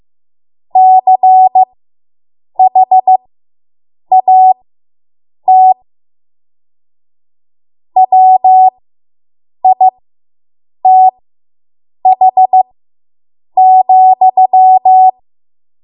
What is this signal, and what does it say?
Morse code, heard as a keyed tone, about 15 wpm, spaced as for 7 wpm, 750 Hz. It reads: CHAT WITH,